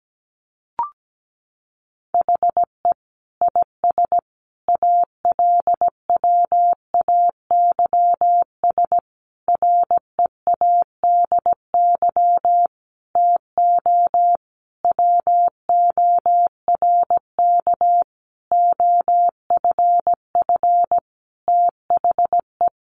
Morse code, heard as a keyed tone, about 17 wpm, 700 Hz. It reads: HE IS ALWAYS READY TO WORK OFF THE
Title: Sentences from Top 1300 Words 17wpm